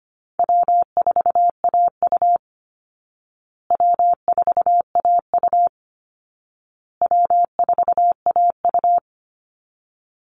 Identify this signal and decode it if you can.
Morse code, heard as a keyed tone, about 25 wpm, 700 Hz. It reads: W4AU W4AU W4AU
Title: Easy Call Signs Repeated 3x - 25wpm